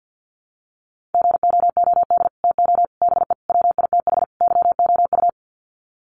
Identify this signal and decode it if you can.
Morse code, heard as a keyed tone, about 49 wpm, 700 Hz. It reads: GOOD TO BE WITH YOU
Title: Sets of 5 Words Encore2 50wpm